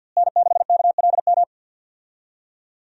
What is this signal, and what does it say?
Morse code, heard as a keyed tone, about 52 wpm, 700 Hz. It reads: N6KLM
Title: USA Call Signs - 5 Characters - Repeated 2x - 50wpm